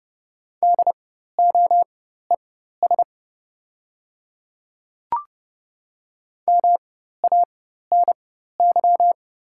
Morse code, heard as a keyed tone, about 30 wpm, spaced as for 14 wpm, 700 Hz. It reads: DOES MANY
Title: 4x Character Spacing - Top 200 Words - 30wpm